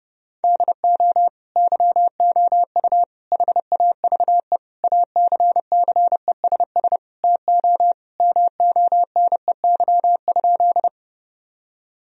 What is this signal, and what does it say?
Morse code, heard as a keyed tone, about 30 wpm, 700 Hz. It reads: DO YOU HAVE ACCESS TO MONEY?
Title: Sentences from Top 1000 Words - Review Part 4 - 30wpm